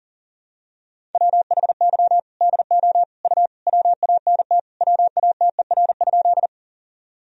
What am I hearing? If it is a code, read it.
Morse code, heard as a keyed tone, about 40 wpm, 700 Hz. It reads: WHY DO U WANT WATER?
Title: Sentences from Top 200 Words and Abbreviations 40wpm